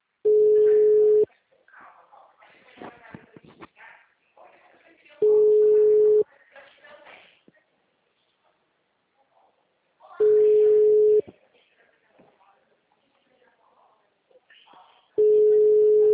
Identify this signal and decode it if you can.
Ringback tone (Europe, 425 Hz), with speech.